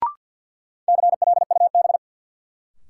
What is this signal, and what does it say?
Morse code, heard as a keyed tone, about 53 wpm, 700 Hz. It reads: CLUB
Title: Sentences from Top 1200 Words - New Word Drill Alt - 50wpm